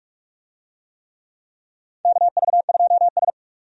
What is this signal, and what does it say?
Morse code, heard as a keyed tone, about 44 wpm, 700 Hz. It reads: KV2S
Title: USA Call Signs - 4 Characters - 45wpm